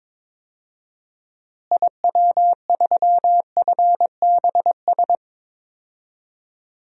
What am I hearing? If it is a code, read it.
Morse code, heard as a keyed tone, about 21 wpm, 700 Hz. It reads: IW3FBS
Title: DXCC-100 Call Signs - 6 Characters - Repeated 2x - 22wpm